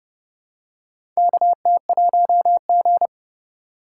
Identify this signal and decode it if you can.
Morse code, heard as a keyed tone, about 30 wpm, 700 Hz. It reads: KT1G
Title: USA Call Signs - 4 Characters - 2x Character Spacing - 30wpm